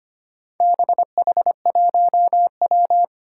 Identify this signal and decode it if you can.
Morse code, heard as a keyed tone, about 25 wpm, 700 Hz. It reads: BH1W
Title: Four Challenging Letter-Number Combinations 25wpm